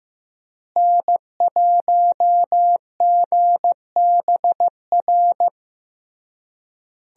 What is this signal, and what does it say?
Morse code, heard as a keyed tone, about 15 wpm, 700 Hz. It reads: N1GBR
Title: USA Call Signs - 5 Characters - Repeated 3x - 15wpm